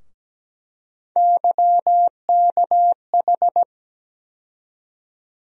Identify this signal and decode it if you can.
Morse code, heard as a keyed tone, about 17 wpm, 700 Hz. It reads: YKH
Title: USA Call Sign Postfixes - 3 Letters 17wpm